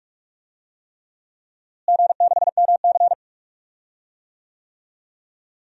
Morse code, heard as a keyed tone, about 44 wpm, 700 Hz. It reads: G6MC
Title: DXCC-100 Call Signs - 4 Characters - Repeated 3x - 45wpm